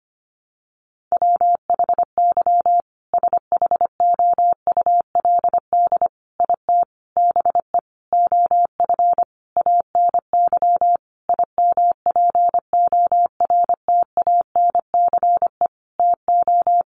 Morse code, heard as a keyed tone, about 25 wpm, 700 Hz. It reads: WHY SHOULD IT BE OF ANY IMPORTANCE TO
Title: Sentences from Top 1200 Words 25wpm